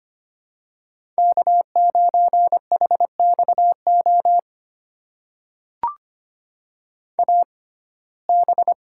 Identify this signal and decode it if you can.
Morse code, heard as a keyed tone, about 25 wpm, spaced as for 18 wpm, 700 Hz. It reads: K9HXO AB
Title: USA Call Signs - 5 Characters - 6x Character Spacing - 25wpm